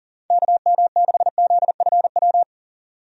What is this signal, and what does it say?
Morse code, heard as a keyed tone, about 40 wpm, 700 Hz. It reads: KM6ZFW
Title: USA Call Signs - 6 Characters - 40wpm